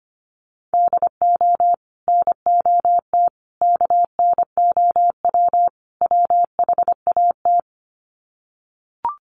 Morse code, heard as a keyed tone, about 25 wpm, 700 Hz. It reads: DO NOT KNOW WHAT
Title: Sets of 4 Words 25wpm